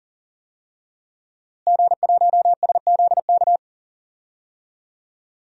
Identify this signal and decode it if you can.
Morse code, heard as a keyed tone, about 40 wpm, 700 Hz. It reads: G1SZK
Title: DXCC-100 Call Signs - 5 Characters - Repeated 2x - 40wpm